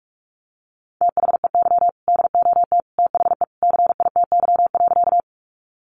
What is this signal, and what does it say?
Morse code, heard as a keyed tone, about 44 wpm, 700 Hz. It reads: THEY DOT THE CITY.
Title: Sentences from Top 1500 Words 45wpm